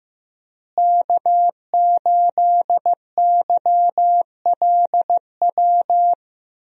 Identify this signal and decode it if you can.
Morse code, heard as a keyed tone, about 15 wpm, 700 Hz. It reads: K8YLW
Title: USA Call Signs - 5 Characters - Repeated 3x - 15wpm